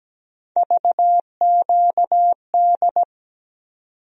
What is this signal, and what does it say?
Morse code, heard as a keyed tone, about 17 wpm, 700 Hz. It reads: VQD
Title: USA Call Sign Postfixes - 3 Letters 17wpm